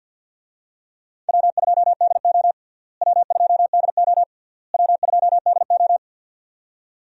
Morse code, heard as a keyed tone, about 47 wpm, 700 Hz. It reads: W2DO W2DO W2DO
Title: USA Call Signs - 4 Characters - Repeated 3x - 50wpm